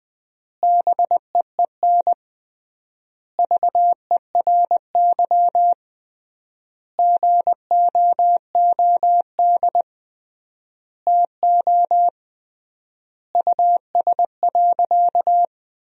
Morse code, heard as a keyed tone, about 20 wpm, 700 Hz. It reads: BEEN VERY GOOD TO US.
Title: 3x Word-Spacing - Sentences from Top 200 Words - 20wpm